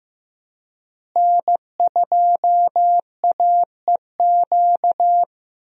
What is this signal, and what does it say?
Morse code, heard as a keyed tone, about 15 wpm, 700 Hz. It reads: N2AEQ
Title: USA Call Signs - 5 Characters - Repeated 3x - 15wpm